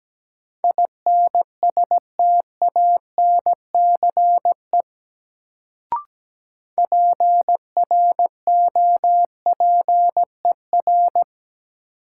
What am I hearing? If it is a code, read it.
Morse code, heard as a keyed tone, about 17 wpm, 700 Hz. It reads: INSTANCE PROPER
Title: Sentences from Top 800 Words - New Word Drill - 17wpm